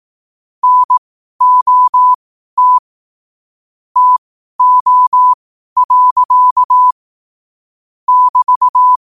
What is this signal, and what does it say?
Morse code, heard as a keyed tone, about 18 wpm, spaced as for 12 wpm, 1000 Hz. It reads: NOT TO. <BT>
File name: qotdfiles / 12wpm